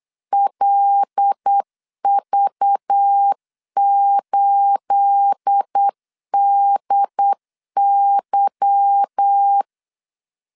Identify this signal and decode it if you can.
Morse code, heard as a keyed tone, about 8 wpm, 800 Hz. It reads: LV8DY
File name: LEZIONI / 9 28 8wpm 3512